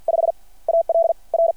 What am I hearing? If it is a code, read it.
Morse code, heard as a keyed tone, about 53 wpm, 670 Hz. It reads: 5 AL R
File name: archived / CRSnet